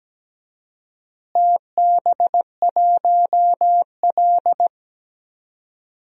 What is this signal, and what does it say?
Morse code, heard as a keyed tone, about 17 wpm, 700 Hz. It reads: TB1L
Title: DXCC-100 Call Signs - 4 Characters - Repeated 2x - 17wpm